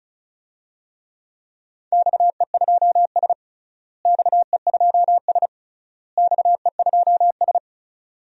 Morse code, heard as a keyed tone, about 36 wpm, 700 Hz. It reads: XE2S XE2S XE2S
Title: DXCC-100 Call Signs - 4 Characters - Repeated 3x - 35wpm